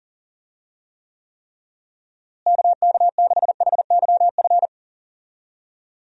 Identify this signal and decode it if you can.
Morse code, heard as a keyed tone, about 40 wpm, 700 Hz. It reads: KK6HYF
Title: USA Call Signs - 6 Characters - Repeated 3x - 40wpm